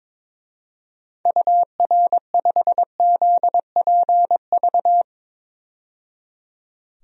Morse code, heard as a keyed tone, about 21 wpm, 700 Hz. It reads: UR5ZPV
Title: DXCC-100 Call Signs - 6 Characters - 22wpm